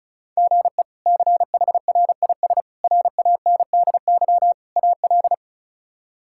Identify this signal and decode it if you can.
Morse code, heard as a keyed tone, about 34 wpm, 700 Hz. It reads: GE CHRIS RANDY AL
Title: K1USN SST QSO Practice - 35wpm